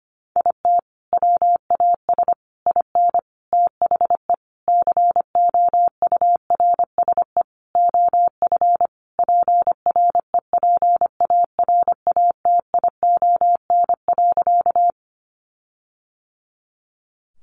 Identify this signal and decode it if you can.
Morse code, heard as a keyed tone, about 25 wpm, 700 Hz. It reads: IT WAS IN THE COURSE OF PREPARATION.
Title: Sentences from Top 1400 Words 25wpm